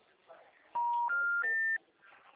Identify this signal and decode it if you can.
Special information tone, with speech.
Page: Araras (FGTS)